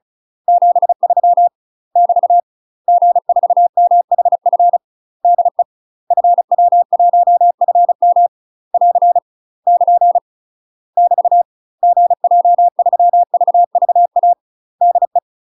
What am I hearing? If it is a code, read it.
Morse code, heard as a keyed tone, about 36 wpm, 700 Hz. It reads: 73 <BT> G4MHF DE FW1FM <AR> <KN> <BT> GJ3VVA DE